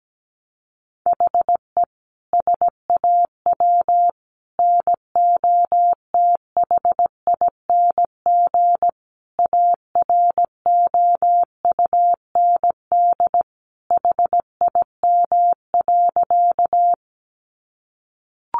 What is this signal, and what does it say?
Morse code, heard as a keyed tone, about 17 wpm, 700 Hz. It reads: HE SAW NOTHING AROUND HIM.
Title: Sentences from Top 600 Words 17wpm